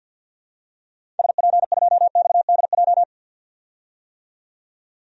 Morse code, heard as a keyed tone, about 51 wpm, 700 Hz. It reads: SP2XDJ
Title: DXCC-100 Call Signs - 6 Characters - 50wpm